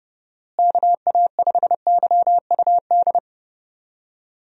Morse code, heard as a keyed tone, about 30 wpm, 700 Hz. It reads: KA5YUD
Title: USA Call Signs - 6 Characters - Repeated 2x - 30wpm